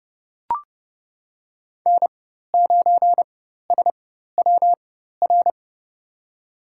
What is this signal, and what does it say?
Morse code, heard as a keyed tone, about 29 wpm, spaced as for 14 wpm, 700 Hz. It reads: N9SWR
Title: USA Call Signs - 4x Character Spacing - 30wpm